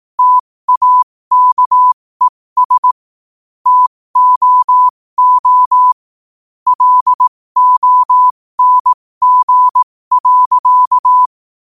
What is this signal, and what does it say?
Morse code, heard as a keyed tone, about 18 wpm, spaced as for 15 wpm, 1000 Hz. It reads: TAKES TOO LONG.
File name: qotdfiles / 15wpm